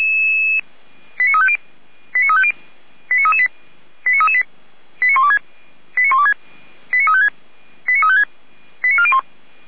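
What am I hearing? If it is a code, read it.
Five-tone selective call (ZVEI), heard as quick runs of five tones, tones of about 70 ms. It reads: 89350 89350 89338 89338 89126 89126 89356 89356 89401
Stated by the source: ZVEI five-tone selective call received by SDR.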